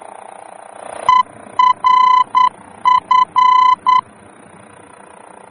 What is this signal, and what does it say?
Morse code, heard as a keyed tone, about 9 wpm, 1010 Hz. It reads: ERF
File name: germany / mx / ERF-D-VOR